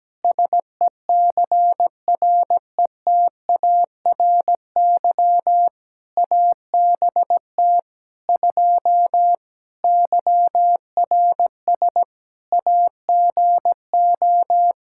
Morse code, heard as a keyed tone, about 17 wpm, 700 Hz. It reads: SECRETARY ABT 2 YRS AGO
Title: Practice QSO - Exchange 3 - More Info - 17wpm